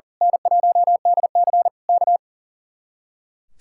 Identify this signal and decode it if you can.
Morse code, heard as a keyed tone, about 37 wpm, 700 Hz. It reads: N1DC K
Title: Speed-Racing Common QSO Phrases - 40wpm